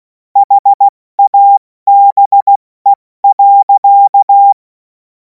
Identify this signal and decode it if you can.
Morse code, heard as a keyed tone, about 16 wpm, spaced as for 14 wpm, 800 Hz. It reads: HABE.
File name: Morsekurs / Witze / Witz 02. Juni 2026 70 F80